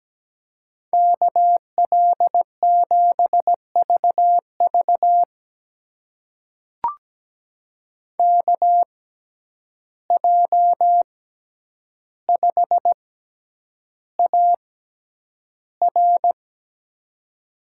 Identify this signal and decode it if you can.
Morse code, heard as a keyed tone, about 17 wpm, spaced as for 9 wpm, 700 Hz. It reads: KL7VV KJ5AR
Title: USA Call Signs - 5 Characters - 6x Character Spacing - 17wpm